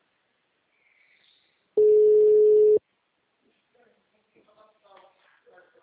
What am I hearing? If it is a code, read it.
Ringback tone (Europe, 425 Hz), with speech.
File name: out- -2036- - - .57